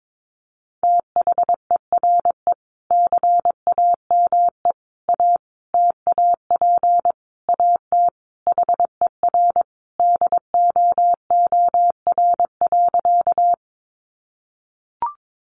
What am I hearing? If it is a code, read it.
Morse code, heard as a keyed tone, about 22 wpm, 700 Hz. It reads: THERE CAME A TAP AT HER DOOR.
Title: Sentences from Top 1400 Words 22wpm